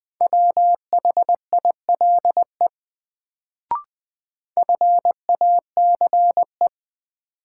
Morse code, heard as a keyed tone, about 20 wpm, 700 Hz. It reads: WHILE FACE
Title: Sentences from Top 300 Words - New Word Drill - 20wpm